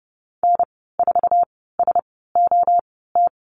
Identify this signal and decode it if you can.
Morse code, heard as a keyed tone, about 30 wpm, spaced as for 17 wpm, 700 Hz. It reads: N4SOT
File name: Lesson 14g - Number 2 - Call Signs 30wpm